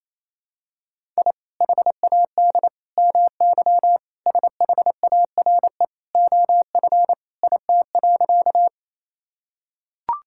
Morse code, heard as a keyed tone, about 27 wpm, 700 Hz. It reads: I HAD MY SHARE OF IT.
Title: Sentences from Top 1000 Words - Review Part 6 - 28wpm